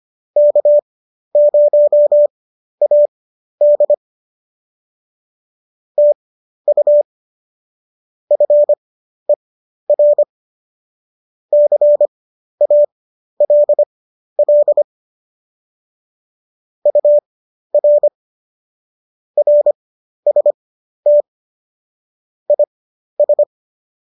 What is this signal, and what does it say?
Morse code, heard as a keyed tone, about 25 wpm, spaced as for 12 wpm, 590 Hz. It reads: K0AD TU FER CALL UR RST IS
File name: qso008 12